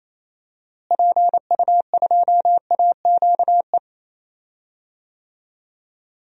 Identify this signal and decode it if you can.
Morse code, heard as a keyed tone, about 29 wpm, 700 Hz. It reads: PU2AQE
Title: DXCC-100 Call Signs - 6 Characters - Repeated 2x - 28wpm